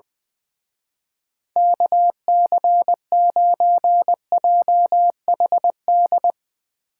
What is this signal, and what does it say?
Morse code, heard as a keyed tone, about 21 wpm, 700 Hz. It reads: KC9JHD
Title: Speed Racing - USA 2x3 Call Signs 20wpm